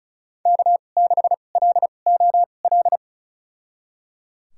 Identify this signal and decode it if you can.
Morse code, heard as a keyed tone, about 36 wpm, spaced as for 25 wpm, 700 Hz. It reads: K6LOL
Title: USA Call Signs - 5 Characters - 2x Character Spacing - 35wpm